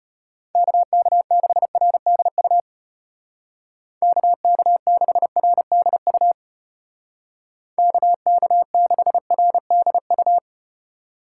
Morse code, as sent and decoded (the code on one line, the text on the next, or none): -.- -.- -.... .-. -.. ..- / -.- -.- -.... .-. -.. ..- / -.- -.- -.... .-. -.. ..-
KK6RDU KK6RDU KK6RDU